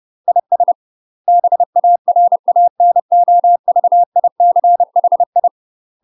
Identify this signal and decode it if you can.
Morse code, heard as a keyed tone, about 30 wpm, 700 Hz. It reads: IS BARANOVICHI
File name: UK MASTER QSOs TRAINING / 30WPM / UK MASTER QSOs 0036